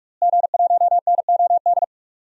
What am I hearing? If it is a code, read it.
Morse code, heard as a keyed tone, about 41 wpm, 700 Hz. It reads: G1NOD
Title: DXCC-100 Call Signs - 5 Characters - Repeated 2x - 45wpm